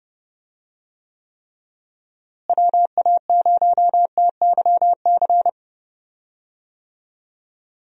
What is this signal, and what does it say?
Morse code, heard as a keyed tone, about 30 wpm, 700 Hz. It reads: WA0TYC
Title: USA Call Signs - 6 Characters - Repeated 3x - 30wpm